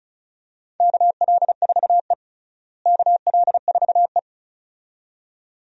Morse code, heard as a keyed tone, about 34 wpm, 700 Hz. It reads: KL4E KL4E
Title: USA Call Signs - 4 Characters - Repeated 2x - 35wpm